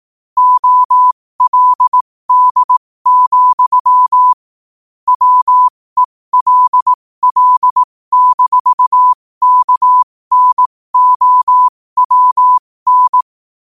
Morse code, heard as a keyed tone, about 18 wpm, spaced as for 15 wpm, 1000 Hz. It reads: OLD, WELL-KNOWN
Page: New quotes every day in morse code at 15 Words per minute.